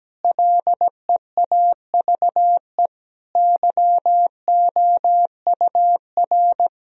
Morse code, heard as a keyed tone, about 17 wpm, 700 Hz. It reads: LEAVE YOUR
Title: Sentences from Top 1000 Words - Review Part 3 - 17wpm